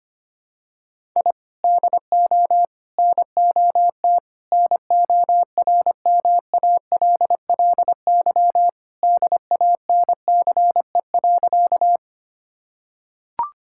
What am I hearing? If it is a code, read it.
Morse code, heard as a keyed tone, about 25 wpm, 700 Hz. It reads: I DO NOT NORMALLY DANCE.
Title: Sentences from Top 1000 Words - Review Part 6 - 25wpm